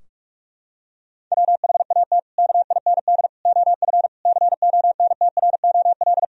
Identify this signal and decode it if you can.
Morse code, heard as a keyed tone, about 46 wpm, 700 Hz. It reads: WHAT KIND OF CONTROL
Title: Sentences from Top 1000 Words - Review Part 3 - 45wpm